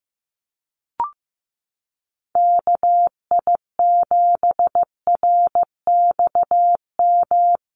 Morse code, heard as a keyed tone, about 15 wpm, 700 Hz. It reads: KI7RXM
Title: USA Call Signs 15wpm